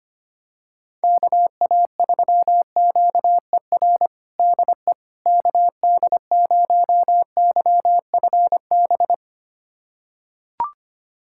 Morse code, heard as a keyed tone, about 25 wpm, 700 Hz. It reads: KA3QER DE KD0YFB
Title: USA Callsign DE Callsign - 6 Characters - 25wpm